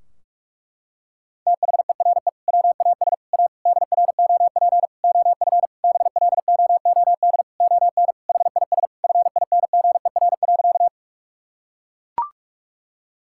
Morse code, heard as a keyed tone, about 41 wpm, 700 Hz. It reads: THERE WAS A DROP OF BLOOD ON HIS FINGER.
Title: Sentences from Top 1600 Words 45wpm